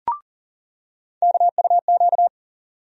Morse code, heard as a keyed tone, about 40 wpm, 700 Hz. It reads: KUQ
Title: USA Call Sign Postfixes - 3 Letters 40wpm